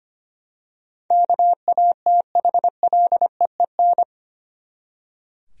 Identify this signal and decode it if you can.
Morse code, heard as a keyed tone, about 25 wpm, 700 Hz. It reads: KATHLEEN
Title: USA Common Names - 25wpm